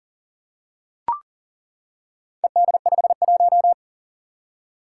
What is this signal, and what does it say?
Morse code, heard as a keyed tone, about 40 wpm, 700 Hz. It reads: ED51
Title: Four Challenging Letter-Number Combinations 40wpm